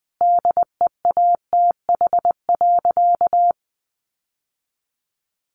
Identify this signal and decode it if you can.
Morse code, heard as a keyed tone, about 20 wpm, 700 Hz. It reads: DEATH.
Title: Sentences from Top 1300 Words 20wpm